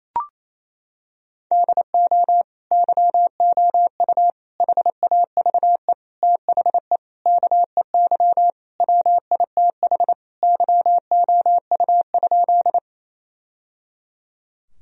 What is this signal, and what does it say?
Morse code, heard as a keyed tone, about 28 wpm, 700 Hz. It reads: DO YOU HAVE THE KEY WITH YOU?
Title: Sentences from Top 1000 Words - Review Part 4 - 28wpm